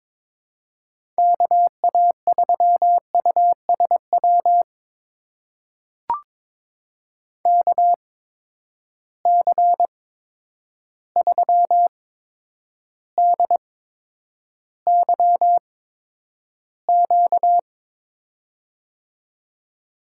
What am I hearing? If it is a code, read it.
Morse code, heard as a keyed tone, about 22 wpm, spaced as for 9 wpm, 700 Hz. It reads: KA3USW KC3DYQ
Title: USA Call Signs - 6 Characters - 8x Character Spacing - 22wpm